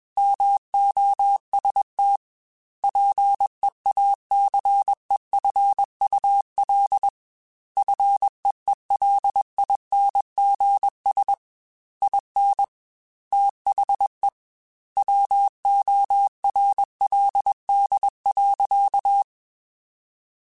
Morse code, heard as a keyed tone, about 21 wpm, 800 Hz. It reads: MOST PEACEFUL FEELINGS IN THE WORLD.
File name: Quotes / Quotes-mp3s / ChildAsleep @20